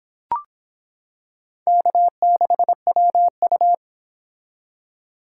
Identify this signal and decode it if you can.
Morse code, heard as a keyed tone, about 26 wpm, 700 Hz. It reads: K6WU